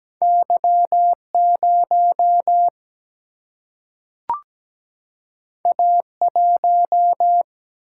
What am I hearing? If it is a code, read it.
Morse code, heard as a keyed tone, about 17 wpm, 700 Hz. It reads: Y0 A1
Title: Single-Letter Single-Number 17wpm